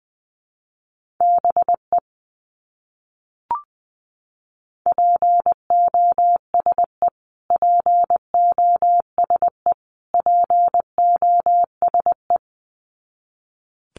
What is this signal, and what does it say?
Morse code, heard as a keyed tone, about 19 wpm, 700 Hz. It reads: BE POSE POSE POSE
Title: Top 500 Words Repeated 20wpm